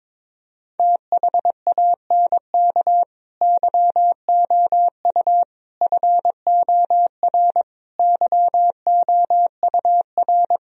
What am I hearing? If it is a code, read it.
Morse code, heard as a keyed tone, about 22 wpm, 700 Hz. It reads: THANK YOU FOR YOUR
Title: Sentences from Top 1000 Words - Review Part 3 - 22wpm